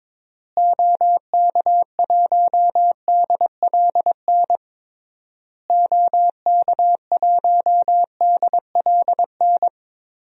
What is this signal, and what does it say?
Morse code, heard as a keyed tone, about 22 wpm, 700 Hz. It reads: OK1DLN OK1DLN
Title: DXCC-100 Call Signs - 6 Characters - Repeated 3x - 22wpm